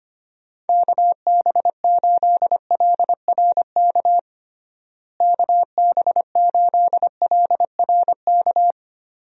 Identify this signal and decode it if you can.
Morse code, heard as a keyed tone, about 25 wpm, 700 Hz. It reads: KB8LRK KB8LRK
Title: USA Call Signs - 6 Characters - Repeated 3x - 25wpm